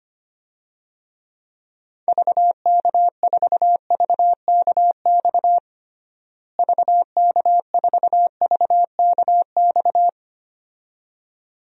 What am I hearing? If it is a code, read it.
Morse code, heard as a keyed tone, about 25 wpm, 700 Hz. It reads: VK4VKX VK4VKX
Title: DXCC-100 Call Signs - 6 Characters - Repeated 2x - 25wpm